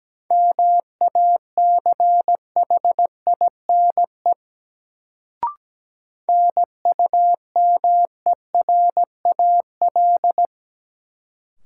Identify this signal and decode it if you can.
Morse code, heard as a keyed tone, about 17 wpm, 700 Hz. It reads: MACHINE NUMERAL
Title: Sentences from Top 1000 Words - Review Part 2 - New Word Drill - 17wpm